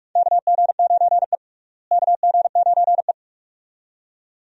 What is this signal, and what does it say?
Morse code, heard as a keyed tone, about 46 wpm, 700 Hz. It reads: KG9E KG9E
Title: USA Call Signs - 4 Characters - Repeated 2x - 45wpm